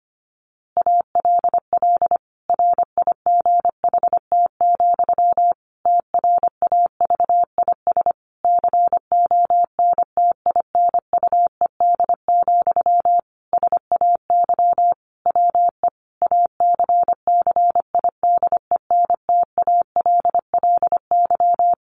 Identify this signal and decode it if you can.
Morse code, heard as a keyed tone, about 25 wpm, 700 Hz. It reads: ALL RIGHT, TRAVIS CONTINUED, SAY WE ACCIDENTALLY
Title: A Sound Of Thunder 25wpm